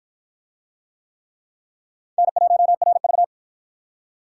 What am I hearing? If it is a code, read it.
Morse code, heard as a keyed tone, about 47 wpm, 700 Hz. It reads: N1RV